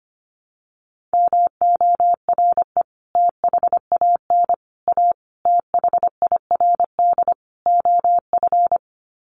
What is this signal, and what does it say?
Morse code, heard as a keyed tone, about 25 wpm, 700 Hz. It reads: MORE THAN A THIRD OF
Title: Sets of 5 Words Encore2 25wpm